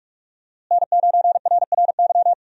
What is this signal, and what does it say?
Morse code, heard as a keyed tone, about 44 wpm, 700 Hz. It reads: N9RRY
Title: USA Call Signs - 5 Characters - Repeated 2x - 45wpm